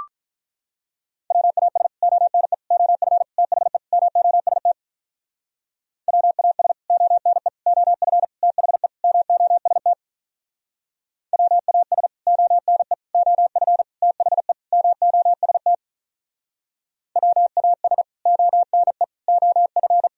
Morse code, heard as a keyed tone, about 45 wpm, 700 Hz. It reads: WAS ONE OF THE MOST WAS ONE OF THE MOST WAS ONE OF THE MOST WAS ONE OF